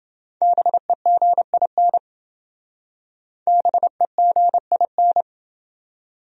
Morse code, heard as a keyed tone, about 28 wpm, 700 Hz. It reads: BEGIN BEGIN